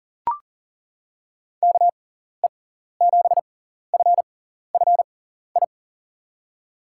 Morse code, heard as a keyed tone, about 40 wpm, spaced as for 14 wpm, 700 Hz. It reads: KE7FFI
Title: USA Call Signs - 6x Character Spacing - 40wpm